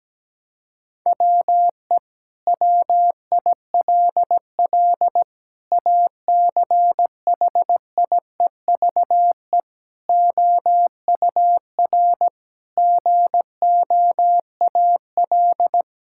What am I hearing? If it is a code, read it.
Morse code, heard as a keyed tone, about 17 wpm, 700 Hz. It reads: WE WILL ACHIEVE OUR GOAL
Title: Sentences from Top 1000 Words - Review Part 6 - 17wpm